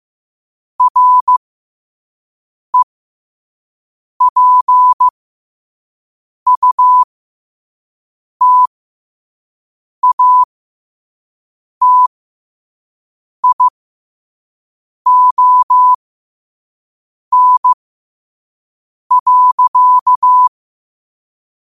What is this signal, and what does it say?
Morse code, heard as a keyed tone, about 15 wpm, spaced as for 5 wpm, 1000 Hz. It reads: REPUTATION.
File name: qotdfiles / 5wpm